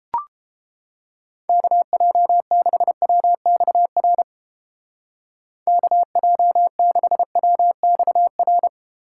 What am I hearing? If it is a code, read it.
Morse code, heard as a keyed tone, about 29 wpm, 700 Hz. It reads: KJ6WXR KJ6WXR